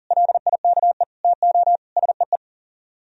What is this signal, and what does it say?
Morse code, heard as a keyed tone, about 49 wpm, 700 Hz. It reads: LIKE TO SEE
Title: Speed-Racing - Sets of 5 Words 40wpm